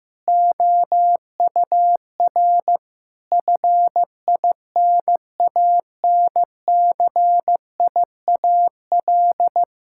Morse code, heard as a keyed tone, about 15 wpm, 700 Hz. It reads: OUR FINANCIAL
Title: Sentences from Top 1000 Words - Review Part 5 - 15wpm